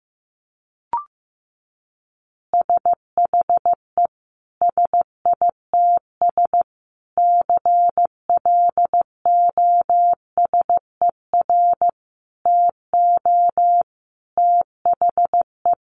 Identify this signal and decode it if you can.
Morse code, heard as a keyed tone, about 15 wpm, 700 Hz. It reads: SHE SITS CLOSER TO THE
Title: Additional Easy Phrases 15wpm